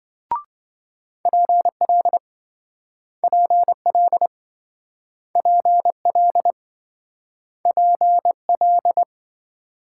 Morse code, heard as a keyed tone, about 26 wpm, 700 Hz. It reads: PL PL PL PL